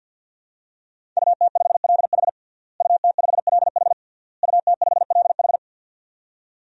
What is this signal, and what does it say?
Morse code, heard as a keyed tone, about 52 wpm, 700 Hz. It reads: UT5LH UT5LH UT5LH
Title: DXCC-100 Call Signs - 5 Characters - Repeated 3x - 50wpm